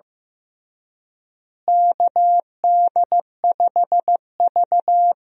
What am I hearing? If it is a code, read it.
Morse code, heard as a keyed tone, about 15 wpm, 700 Hz. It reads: KD5V
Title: USA Call Signs - 4 Characters - Repeated 3x - 15wpm